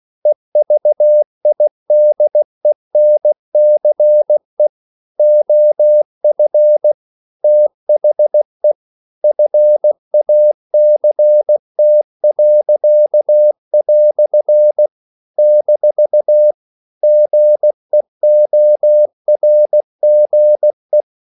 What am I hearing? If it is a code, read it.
Morse code, heard as a keyed tone, about 16 wpm, 600 Hz. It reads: EVIDENCE OF THE FACT." - GEORGE
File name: CW / Practice / EnglishQuotations0216